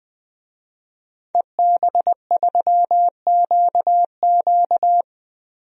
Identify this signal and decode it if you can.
Morse code, heard as a keyed tone, about 20 wpm, 700 Hz. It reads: EB3QQ
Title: DXCC-100 Call Signs - 5 Characters - Repeated 2x - 20wpm